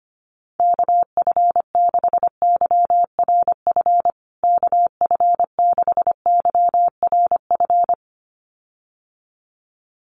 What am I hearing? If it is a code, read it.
Morse code, heard as a keyed tone, about 25 wpm, 700 Hz. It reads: KF6YRF KF6YRF
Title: USA Call Signs Repeated 2x 25wpm